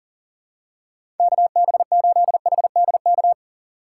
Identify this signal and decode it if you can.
Morse code, heard as a keyed tone, about 40 wpm, 700 Hz. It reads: KB8HDK
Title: USA Call Signs - 6x Character Spacing - 40wpm